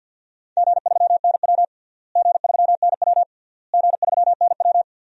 Morse code, heard as a keyed tone, about 52 wpm, 700 Hz. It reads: G3NW G3NW G3NW
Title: DXCC-100 Call Signs - 4 Characters - Repeated 3x - 50wpm